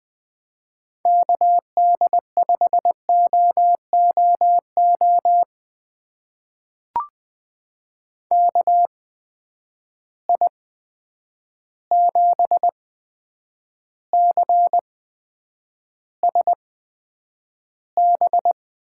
Morse code, heard as a keyed tone, about 20 wpm, spaced as for 9 wpm, 700 Hz. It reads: KD5OOO KI7CSB
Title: USA Call Signs - 6 Characters - 8x Character Spacing - 20wpm